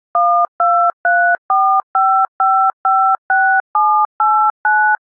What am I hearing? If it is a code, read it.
DTMF, heard as telephone touch tones, tones of about 330 ms. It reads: 12345556789